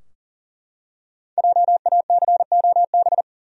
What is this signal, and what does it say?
Morse code, heard as a keyed tone, about 40 wpm, 700 Hz. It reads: JACOB
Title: USA Common Names - Max 5 Letters - 40wpm